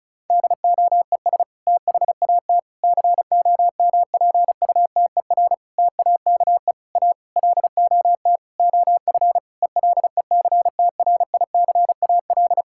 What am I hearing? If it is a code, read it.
Morse code, heard as a keyed tone, about 36 wpm, 700 Hz. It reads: DOES THAT COMPUTER TAKE A LOT OF ELECTRICAL
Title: Sentences from Top 1000 Words - Review Part 6 - 35wpm